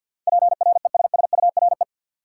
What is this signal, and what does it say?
Morse code, heard as a keyed tone, about 48 wpm, 700 Hz. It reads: PRESSURE
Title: Sentences from Top 1000 Words - Review Part 6 - New Word Drill - 50wpm